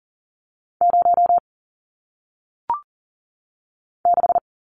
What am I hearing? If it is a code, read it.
Morse code, heard as a keyed tone, about 39 wpm, 700 Hz. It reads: 0 6
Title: Numbers - 1-Digit 40wpm